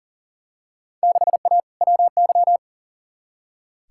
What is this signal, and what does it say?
Morse code, heard as a keyed tone, about 40 wpm, 700 Hz. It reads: 6A WY
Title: Field Day Exchanges Encore - 40wpm